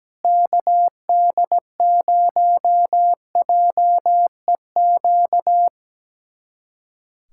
Morse code, heard as a keyed tone, about 17 wpm, 700 Hz. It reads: KD0JEQ
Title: USA Callsign DE Callsign 17wpm